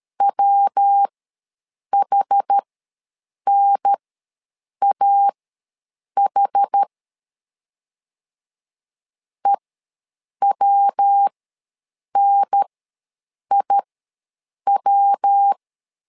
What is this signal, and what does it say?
Morse code, heard as a keyed tone, about 13 wpm, spaced as for 7 wpm, 800 Hz. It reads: WHNAH EWNIW
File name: LEZIONI / 3 MWRO / 3 MW ws 12wpm 1816